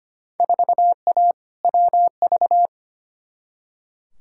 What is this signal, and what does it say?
Morse code, heard as a keyed tone, about 25 wpm, 700 Hz. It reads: 4A WV
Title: Field Day Exchanges Encore - 25wpm